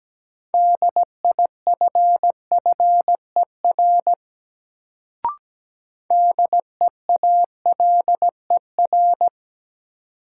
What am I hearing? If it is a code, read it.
Morse code, heard as a keyed tone, about 17 wpm, 700 Hz. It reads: DIFFER DEALER
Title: Sentences from Top 1100 Words - New Word Drill Alt - 17wpm